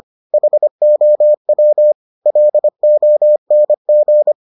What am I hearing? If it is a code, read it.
Morse code, heard as a keyed tone, about 25 wpm, 600 Hz. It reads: HOW LONG